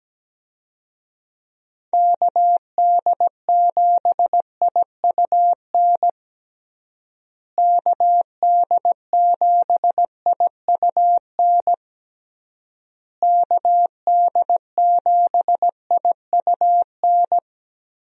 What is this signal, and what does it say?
Morse code, heard as a keyed tone, about 17 wpm, 700 Hz. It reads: KD7IUN KD7IUN KD7IUN
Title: USA Call Signs - 6 Characters - Repeated 3x - 17wpm